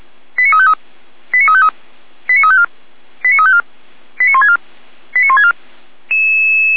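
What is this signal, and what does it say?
Five-tone selective call (ZVEI), heard as quick runs of five tones, tones of about 70 ms. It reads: 89353 89353 89354 89354 89164 89164
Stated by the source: ZVEI five-tone selective call received on an SDR.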